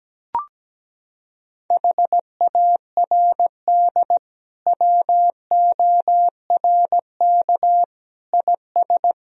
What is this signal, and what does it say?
Morse code, heard as a keyed tone, about 17 wpm, 700 Hz. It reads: HARD WORK IS
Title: Sentences from Top 1000 Words - Review Part 1 - 17wpm